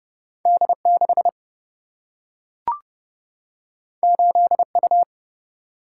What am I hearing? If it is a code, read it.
Morse code, heard as a keyed tone, about 30 wpm, 700 Hz. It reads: D6 8U
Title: Single-Letter Single-Number 30wpm